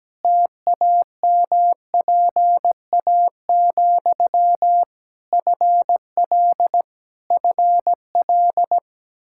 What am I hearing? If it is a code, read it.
Morse code, heard as a keyed tone, about 17 wpm, 700 Hz. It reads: TAMPA, FL FL
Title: K1USN SST QSO Practice 17wpm